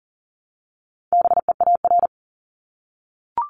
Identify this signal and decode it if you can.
Morse code, heard as a keyed tone, about 42 wpm, 690 Hz. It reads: BEAR
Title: Multi-letter Phonemes 40wpm